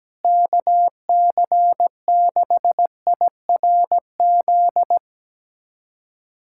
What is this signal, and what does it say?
Morse code, heard as a keyed tone, about 17 wpm, 700 Hz. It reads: KC6IRZ
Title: USA Callsign DE Callsign 17wpm